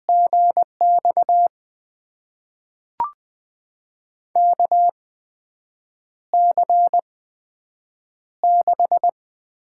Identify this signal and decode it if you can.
Morse code, heard as a keyed tone, about 20 wpm, spaced as for 7 wpm, 700 Hz. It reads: GX KC6
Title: USA Call Signs - 6 Characters - 8x Character Spacing - 20wpm